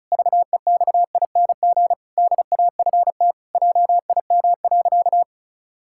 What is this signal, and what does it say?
Morse code, heard as a keyed tone, about 34 wpm, 700 Hz. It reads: VEXING DAFT JIM.
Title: Pangram Challenge v2 - 35wpm